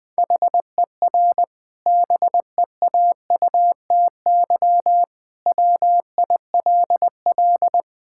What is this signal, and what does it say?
Morse code, heard as a keyed tone, about 21 wpm, 700 Hz. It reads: HER BEAUTY WILL
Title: Sentences from Top 1000 Words - Review Part 5 - 20wpm